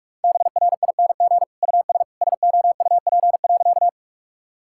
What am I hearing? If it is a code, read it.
Morse code, heard as a keyed tone, about 42 wpm, 700 Hz. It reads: BRING US SOUP.
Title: Sentences from Top 1000 Words - Review Part 6 - 45wpm